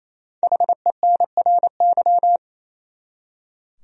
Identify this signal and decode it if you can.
Morse code, heard as a keyed tone, about 28 wpm, 700 Hz. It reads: HENRY
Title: USA Common Names - Max 5 Letters - 28wpm